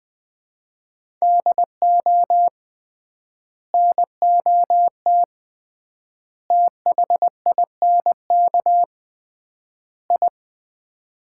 Morse code, heard as a keyed tone, about 20 wpm, 700 Hz. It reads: DO NOT THINK I
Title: 3x Word-Spacing - Sets of 4 Words - 20wpm